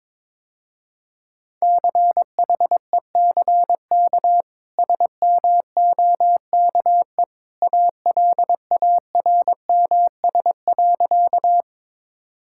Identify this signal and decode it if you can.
Morse code, heard as a keyed tone, about 22 wpm, 700 Hz. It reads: CHECK SMOKE ALARMS.
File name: practice / 22wpm / Easy Sentences with Hints - 22wpm